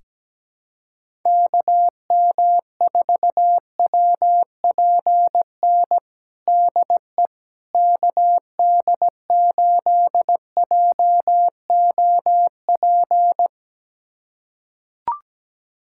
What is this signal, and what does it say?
Morse code, heard as a keyed tone, about 17 wpm, 700 Hz. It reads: KM4WPN DE KD8JOP
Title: USA Callsign DE Callsign - 6 Characters - 17wpm